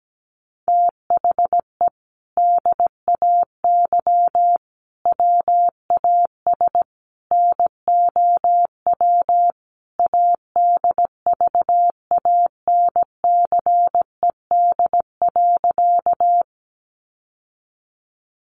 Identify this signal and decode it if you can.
Morse code, heard as a keyed tone, about 17 wpm, 700 Hz. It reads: THE DAY WAS NOW ADVANCED.
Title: Sentences from Top 1200 Words 17wpm